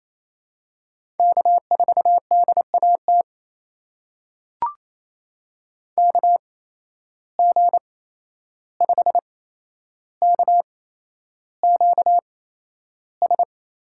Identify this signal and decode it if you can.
Morse code, heard as a keyed tone, about 28 wpm, spaced as for 11 wpm, 700 Hz. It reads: K4DAT KG5KQS
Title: USA Call Signs - 8x Character Spacing - 28wpm